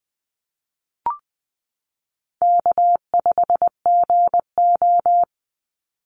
Morse code, heard as a keyed tone, about 20 wpm, 700 Hz. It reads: K5GO
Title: Easy Call Signs Repeated 2x - 20wpm